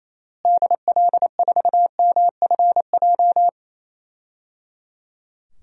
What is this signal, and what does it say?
Morse code, heard as a keyed tone, about 28 wpm, 700 Hz. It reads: DL4MFJ
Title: DXCC-100 Call Signs - 6 Characters - Repeated 3x - 28wpm